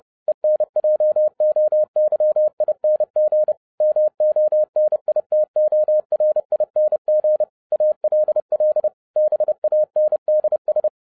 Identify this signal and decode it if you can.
Morse code, heard as a keyed tone, about 30 wpm, 610 Hz. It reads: ENJOYING MONITORING ALL BANDS
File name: 2011-07-mp3 / n60 BACK FROM A BIG ACCIDENT 30wpm